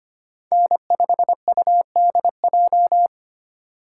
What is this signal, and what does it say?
Morse code, heard as a keyed tone, about 25 wpm, 700 Hz. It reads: N5UDJ
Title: USA Call Signs - 5 Characters - 4x Character Spacing - 25wpm